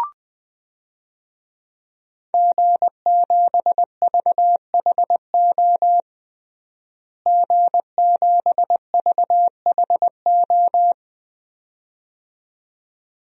Morse code, heard as a keyed tone, about 21 wpm, 700 Hz. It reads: G7VHO G7VHO
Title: DXCC-100 Call Signs - 5 Characters - Repeated 2x - 20wpm